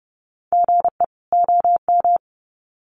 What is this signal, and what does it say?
Morse code, heard as a keyed tone, about 30 wpm, 700 Hz. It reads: GE OM
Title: Common QSO Phrases 30wpm